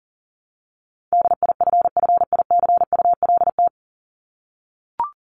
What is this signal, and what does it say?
Morse code, heard as a keyed tone, about 40 wpm, 700 Hz. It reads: DIFFICULT
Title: Top 501-750 Words 40wpm